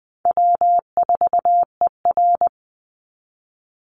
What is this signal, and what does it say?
Morse code, heard as a keyed tone, about 20 wpm, 700 Hz. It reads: W4ER
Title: Easy Call Signs Repeated 2x - 20wpm